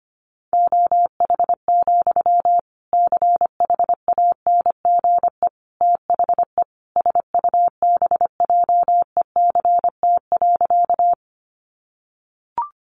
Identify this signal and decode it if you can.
Morse code, heard as a keyed tone, about 25 wpm, 700 Hz. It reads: OH, CHANGE THE SUBJECT.
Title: Sentences from Top 800 Words 25wpm